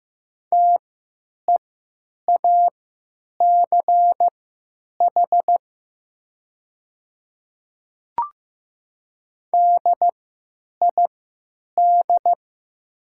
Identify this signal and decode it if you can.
Morse code, heard as a keyed tone, about 15 wpm, spaced as for 9 wpm, 700 Hz. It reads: TEACH DID
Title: 3x Character Spacing - Top 500 Words - 15wpm